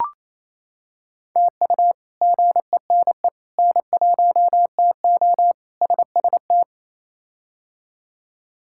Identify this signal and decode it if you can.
Morse code, heard as a keyed tone, about 28 wpm, 700 Hz. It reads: TU GENE N1TO SST
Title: K1USN SST QSO Practice - 28wpm